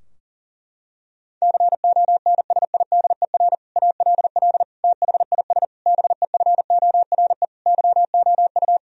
Morse code, heard as a keyed tone, about 40 wpm, 700 Hz. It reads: CONSIDER ALL THIS BEFORE YOU
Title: Sentences from Top 1000 Words - Review Part 4 - 40wpm